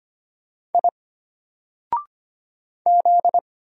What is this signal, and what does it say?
Morse code, heard as a keyed tone, about 25 wpm, 700 Hz. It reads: I Z
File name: Single-Letter-Number-25wpm